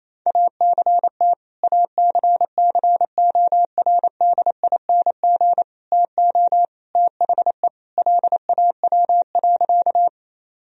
Morse code, heard as a keyed tone, about 28 wpm, 700 Hz. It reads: ACT ACCORDING TO THE LAW.
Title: Sentences from Top 1000 Words - Review Part 4 - 28wpm